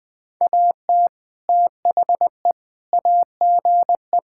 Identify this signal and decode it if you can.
Morse code, heard as a keyed tone, about 20 wpm, 700 Hz. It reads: AT THE AGE
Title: Sentences with 4-Word Phrases - 20wpm